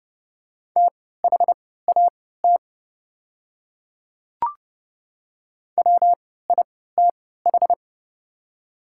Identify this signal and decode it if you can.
Morse code, heard as a keyed tone, about 30 wpm, spaced as for 17 wpm, 700 Hz. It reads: THAT WITH
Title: 3x Character Spacing - Top 100 Words - 30wpm